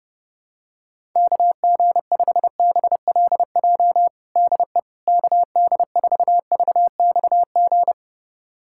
Morse code, heard as a keyed tone, about 32 wpm, 700 Hz. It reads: KG5BLJ DE KD4VXG
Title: USA Callsign DE Callsign 30wpm